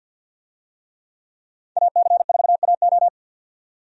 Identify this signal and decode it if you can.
Morse code, heard as a keyed tone, about 52 wpm, 700 Hz. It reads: AC4AO
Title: USA Call Signs - 5 Characters - 50wpm